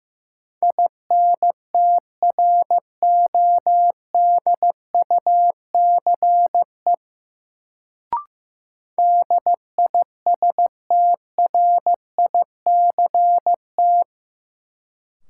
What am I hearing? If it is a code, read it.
Morse code, heard as a keyed tone, about 15 wpm, 700 Hz. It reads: INTRODUCE DISTRICT
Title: Sentences from Top 1600 Words - New Word Drill - 15wpm